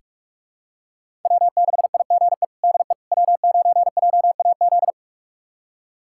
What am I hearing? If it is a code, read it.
Morse code, heard as a keyed tone, about 46 wpm, 700 Hz. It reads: W6IGE DE W9JAZ
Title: USA Callsign DE Callsign - 5 Characters - 45wpm